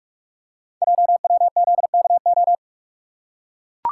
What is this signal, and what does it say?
Morse code, heard as a keyed tone, about 44 wpm, 700 Hz. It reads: JWZKO
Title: Code Groups - 5 Letters - 45wpm